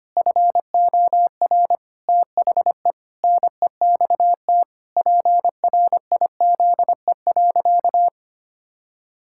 Morse code, heard as a keyed tone, about 25 wpm, 700 Hz. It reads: FOR THE NEXT PRIZE.
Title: Pangram Challenge v2 - 25wpm